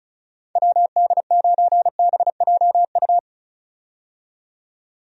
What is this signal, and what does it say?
Morse code, heard as a keyed tone, about 34 wpm, 700 Hz. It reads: WD9BJU
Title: USA Call Signs - 6 Characters - 35wpm